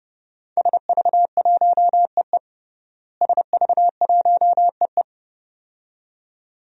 Morse code, heard as a keyed tone, about 29 wpm, 700 Hz. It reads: SV1EE SV1EE
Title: DXCC-100 Call Signs - 5 Characters - Repeated 2x - 30wpm